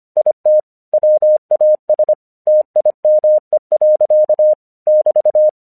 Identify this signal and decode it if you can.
Morse code, heard as a keyed tone, about 25 wpm, 610 Hz. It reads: IT WAS TIME. <BT>
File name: 1 English 25wpm / AQC18